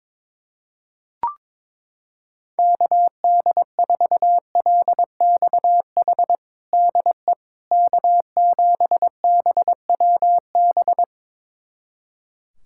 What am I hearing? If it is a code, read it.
Morse code, heard as a keyed tone, about 22 wpm, 700 Hz. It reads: KD4LXH DE K7BWB
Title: USA Callsign DE Callsign 22wpm